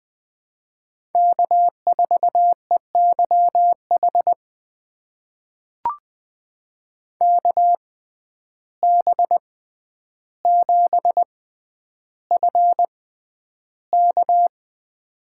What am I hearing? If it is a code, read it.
Morse code, heard as a keyed tone, about 20 wpm, spaced as for 10 wpm, 700 Hz. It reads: K4EYH KB7FK
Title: USA Call Signs - 5 Characters - 6x Character Spacing - 20wpm